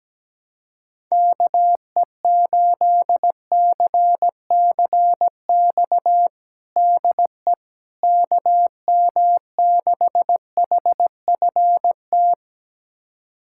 Morse code, heard as a keyed tone, about 17 wpm, 700 Hz. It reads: KE8CCX DE KM6HFT
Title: USA Callsign DE Callsign - 6 Characters - 17wpm